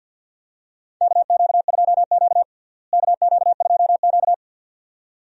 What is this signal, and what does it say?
Morse code, heard as a keyed tone, about 52 wpm, 700 Hz. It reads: KQ2Q KQ2Q
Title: USA Call Signs - 4 Characters - Repeated 2x - 50wpm